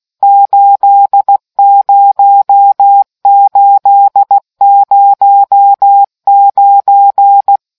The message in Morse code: 80809